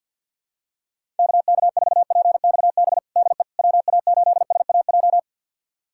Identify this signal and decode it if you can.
Morse code, heard as a keyed tone, about 52 wpm, 700 Hz. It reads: KK4PXB DE WA8SAJ
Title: USA Callsign DE Callsign - 6 Characters - 50wpm